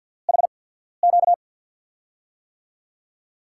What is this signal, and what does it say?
Morse code, heard as a keyed tone, about 52 wpm, spaced as for 14 wpm, 700 Hz. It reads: HQ
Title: USA Call Signs - 6 Characters - 8x Character Spacing - 50wpm